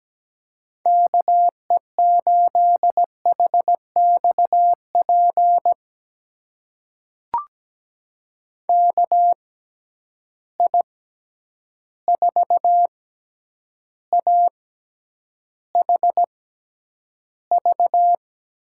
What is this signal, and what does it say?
Morse code, heard as a keyed tone, about 17 wpm, spaced as for 9 wpm, 700 Hz. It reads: KE8HXP KI4AHV
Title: USA Call Signs - 6 Characters - 6x Character Spacing - 17wpm